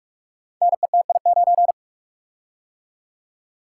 Morse code, heard as a keyed tone, about 44 wpm, 700 Hz. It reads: NETI9
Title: Code Groups - 5 Letter:Numbers - 45wpm